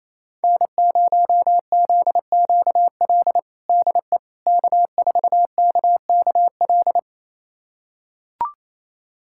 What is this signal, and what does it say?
Morse code, heard as a keyed tone, about 28 wpm, 700 Hz. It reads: N0ZQL DE K4KKL
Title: USA Callsign DE Callsign - 5 Characters - 28wpm